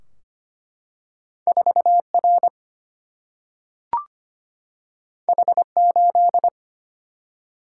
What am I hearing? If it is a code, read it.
Morse code, heard as a keyed tone, about 25 wpm, 700 Hz. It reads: 4R H8
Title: Single-Letter Single-Number 25wpm